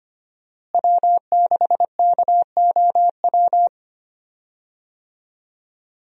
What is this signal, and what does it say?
Morse code, heard as a keyed tone, about 25 wpm, 700 Hz. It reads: W6KOW
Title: USA Call Signs - 5 Characters - Repeated 2x - 25wpm